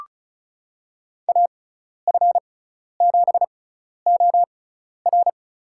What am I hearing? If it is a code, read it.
Morse code, heard as a keyed tone, about 36 wpm, spaced as for 12 wpm, 700 Hz. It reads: AF7OR
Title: USA Call Signs - 5 Characters - 6x Character Spacing - 35wpm